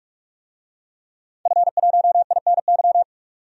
Morse code, heard as a keyed tone, about 41 wpm, 700 Hz. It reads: F1INY
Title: DXCC-100 Call Signs - 5 Characters - 45wpm